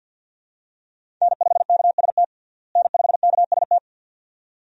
Morse code, heard as a keyed tone, about 52 wpm, 700 Hz. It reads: N5KST N5KST
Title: USA Call Signs - 5 Characters - Repeated 2x - 50wpm